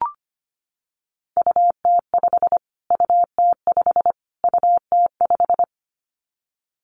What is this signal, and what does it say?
Morse code, heard as a keyed tone, about 25 wpm, 700 Hz. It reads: UT5 UT5 UT5
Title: Callsign Prefixes for Top 25 DXCC - Repeated 3x - 25wpm